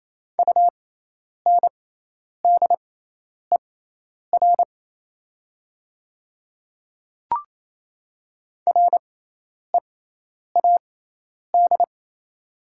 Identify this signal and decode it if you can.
Morse code, heard as a keyed tone, about 28 wpm, spaced as for 10 wpm, 700 Hz. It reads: UNDER READ
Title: 6x Character Spacing - Top 200 Words - 28wpm